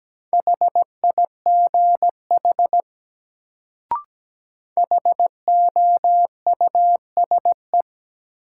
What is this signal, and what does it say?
Morse code, heard as a keyed tone, about 17 wpm, 700 Hz. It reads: HIGH HOUSE
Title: Sets of 2 Words - New Word Drill - 17wpm